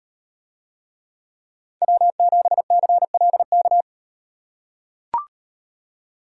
Morse code, heard as a keyed tone, about 38 wpm, 700 Hz. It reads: W7CLK